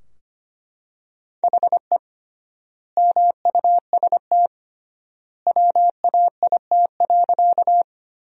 Morse code, heard as a keyed tone, about 25 wpm, 700 Hz. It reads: HE MUST WAIT.
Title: 3x Word-Spacing - Sentences from Top 500 Words - 25wpm